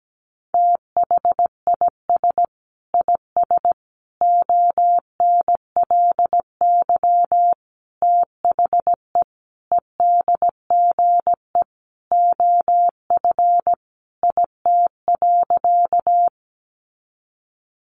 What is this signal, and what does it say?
Morse code, heard as a keyed tone, about 17 wpm, 700 Hz. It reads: THIS IS ONLY THE EDGE OF IT.
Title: Sentences from Top 1300 Words 17wpm